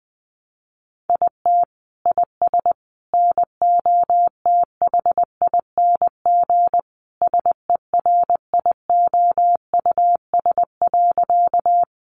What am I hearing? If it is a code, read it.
Morse code, heard as a keyed tone, about 20 wpm, 700 Hz. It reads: IT IS NOTHING SERIOUS.
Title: Sentences from Top 1100 Words 20wpm